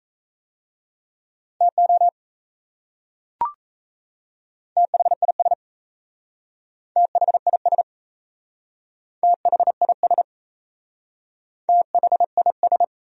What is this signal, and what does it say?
Morse code, heard as a keyed tone, about 37 wpm, 700 Hz. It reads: TO THIS THIS THIS THIS